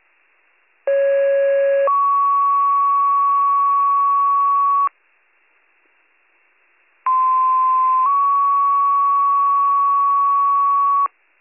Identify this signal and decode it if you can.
Two-tone page, heard as a one-second tone, then a three-second tone, about 550 then 1120 Hz, about 1040 then 1120 Hz. It is heard from 0.9 to 11.1 s.